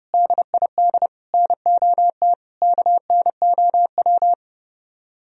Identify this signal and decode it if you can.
Morse code, heard as a keyed tone, about 36 wpm, 700 Hz. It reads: DID NOT KNOW
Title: Speed-Racing - Sets of 4 Words 30wpm